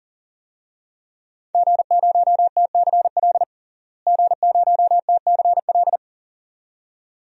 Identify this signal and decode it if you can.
Morse code, heard as a keyed tone, about 40 wpm, 700 Hz. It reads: G0TCL G0TCL
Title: DXCC-100 Call Signs - 5 Characters - Repeated 2x - 40wpm